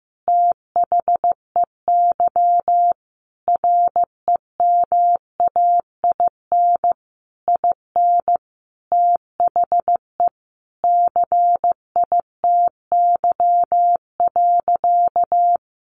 Morse code, heard as a keyed tone, about 15 wpm, 700 Hz. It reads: THEY REMAIN IN THE CITY.
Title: Sentences from Top 1200 Words 15wpm